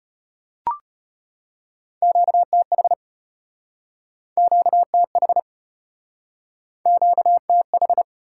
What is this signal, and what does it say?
Morse code, heard as a keyed tone, about 34 wpm, 700 Hz. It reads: QTH QTH QTH